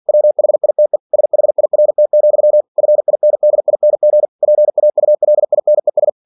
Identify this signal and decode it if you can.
Morse code, heard as a keyed tone, about 47 wpm, 600 Hz. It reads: WHITE SHIRT, FINDING PAULINES
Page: Never 47wpm